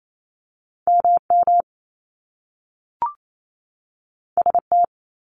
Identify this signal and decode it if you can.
Morse code, heard as a keyed tone, about 28 wpm, 700 Hz. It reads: MM ST
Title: Top Two Letter Combinations in Words 28wpm